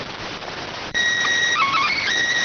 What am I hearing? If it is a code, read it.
SSTV (slow-scan television), PD 290.